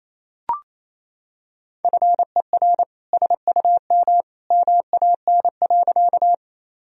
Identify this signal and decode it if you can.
Morse code, heard as a keyed tone, about 28 wpm, 700 Hz. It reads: FER SUM MAN.
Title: Sentences from Top 200 Words and Abbreviations 28wpm